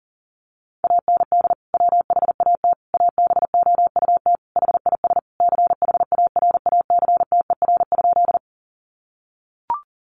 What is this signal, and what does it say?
Morse code, heard as a keyed tone, about 40 wpm, 700 Hz. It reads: AND WHAT ABOUT HIS CHARACTER?
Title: Sentences from Top 1200 Words 40wpm